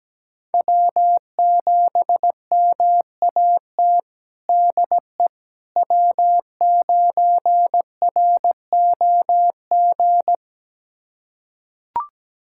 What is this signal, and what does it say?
Morse code, heard as a keyed tone, about 17 wpm, 700 Hz. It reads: W7MAT DE W9ROG
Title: USA Callsign DE Callsign - 5 Characters - 17wpm